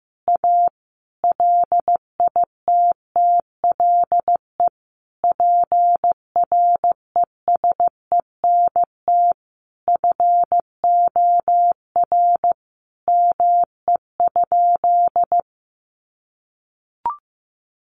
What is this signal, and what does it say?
Morse code, heard as a keyed tone, about 15 wpm, 700 Hz. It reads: A LITTLE PRESENT FOR ME?
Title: Sentences from Top 800 Words 15wpm